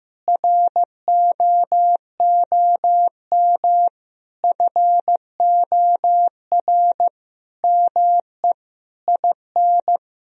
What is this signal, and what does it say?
Morse code, heard as a keyed tone, about 15 wpm, 700 Hz. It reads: ROOM FOR ME IN
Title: Sentences from Top 1000 Words - Review Part 4 - 15wpm